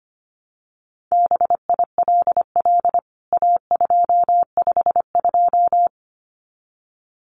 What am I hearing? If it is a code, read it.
Morse code, heard as a keyed tone, about 25 wpm, 700 Hz. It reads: BILL A252
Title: CWT Exchange 25wpm